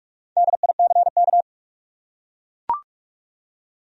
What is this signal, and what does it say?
Morse code, heard as a keyed tone, about 42 wpm, 700 Hz. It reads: DICK
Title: USA Common Names - Max 5 Letters - 45wpm